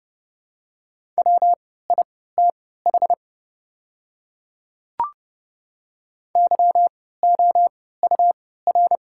Morse code, heard as a keyed tone, about 30 wpm, spaced as for 17 wpm, 700 Hz. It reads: WITH YOUR
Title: 3x Character Spacing - Top 100 Words - 30wpm